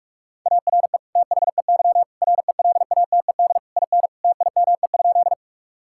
Morse code, heard as a keyed tone, about 41 wpm, 700 Hz. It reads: ARE THEY RELATED IN TIME?
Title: Sentences from Top 1000 Words - Review Part 5 - 45wpm